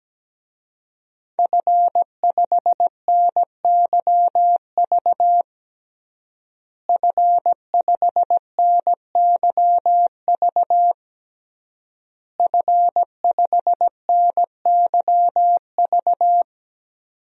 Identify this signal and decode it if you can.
Morse code, heard as a keyed tone, about 17 wpm, 700 Hz. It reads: F5NYV F5NYV F5NYV
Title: DXCC-100 Call Signs - 5 Characters - Repeated 3x - 17wpm